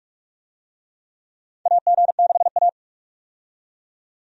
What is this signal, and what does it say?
Morse code, heard as a keyed tone, about 47 wpm, 700 Hz. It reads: AG6A
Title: USA Call Signs - 4 Characters - 45wpm